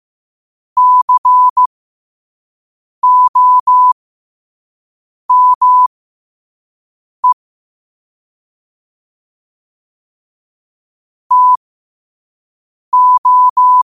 Morse code, heard as a keyed tone, about 15 wpm, spaced as for 5 wpm, 1000 Hz. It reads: COME TO
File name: qotdfiles / 5wpm